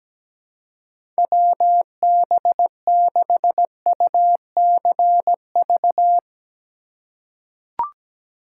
Morse code, heard as a keyed tone, about 17 wpm, 700 Hz. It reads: WB6UCV
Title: USA Call Signs - 6 Characters - 3x Character Spacing - 17wpm